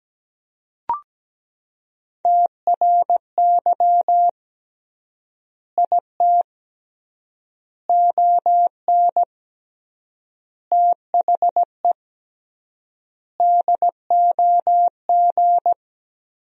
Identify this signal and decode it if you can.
Morse code, heard as a keyed tone, about 17 wpm, 700 Hz. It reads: TRY IT ON THE DOG
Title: 3x Word-Spacing - Sentences from Top 400 Words - 17wpm